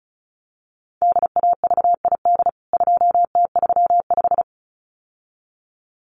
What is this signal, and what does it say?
Morse code, heard as a keyed tone, about 34 wpm, 700 Hz. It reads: DAVID 2T35
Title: CWT Exchange 35wpm